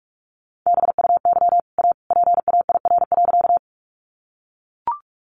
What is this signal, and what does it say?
Morse code, heard as a keyed tone, about 44 wpm, 700 Hz. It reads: BUY A PAIR.
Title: Sentences from Top 1200 Words 45wpm